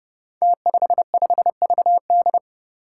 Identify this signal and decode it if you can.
Morse code, heard as a keyed tone, about 31 wpm, 700 Hz. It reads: T55VD
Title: Five Challenging Letter-Number Combinations 30wpm